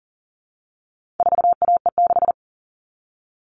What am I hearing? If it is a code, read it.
Morse code, heard as a keyed tone, about 40 wpm, 700 Hz. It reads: 4AE6
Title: All Cut Numbers - 4-Digit 40wpm